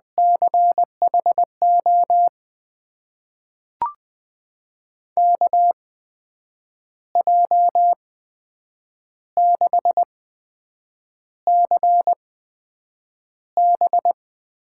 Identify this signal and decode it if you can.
Morse code, heard as a keyed tone, about 20 wpm, spaced as for 7 wpm, 700 Hz. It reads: CHO KJ6CB
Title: USA Call Signs - 6 Characters - 8x Character Spacing - 20wpm